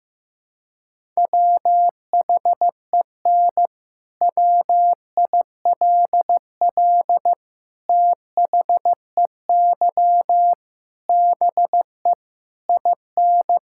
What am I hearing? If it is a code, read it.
Morse code, heard as a keyed tone, about 15 wpm, 700 Hz. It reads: WHEN WILL THEY BE IN
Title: Sentences from Top 1000 Words - Review Part 6 - 15wpm